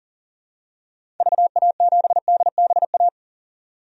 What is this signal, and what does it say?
Morse code, heard as a keyed tone, about 37 wpm, 700 Hz. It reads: VA7DBA
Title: DXCC-100 Call Signs - 6 Characters - 40wpm